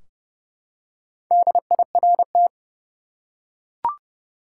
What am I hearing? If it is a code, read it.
Morse code, heard as a keyed tone, about 30 wpm, 700 Hz. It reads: DIRT
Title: Four Letter Words - 30wpm